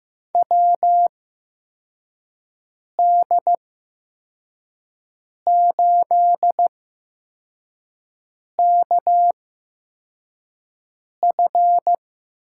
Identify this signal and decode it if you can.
Morse code, heard as a keyed tone, about 15 wpm, spaced as for 4 wpm, 700 Hz. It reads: WD8KF
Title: USA Call Signs - 6 Characters - 8x Character Spacing - 15wpm